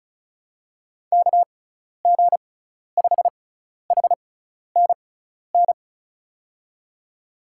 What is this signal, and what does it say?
Morse code, heard as a keyed tone, about 36 wpm, spaced as for 12 wpm, 690 Hz. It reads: KG5HNN
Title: USA Call Signs - 6 Characters - 6x Character Spacing - 35wpm